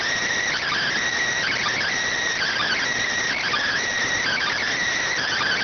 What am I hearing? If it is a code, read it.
Emergency Alert System (SAME) tones, heard as harsh buzzing bursts. It is heard from 0.5 to 4.5 s.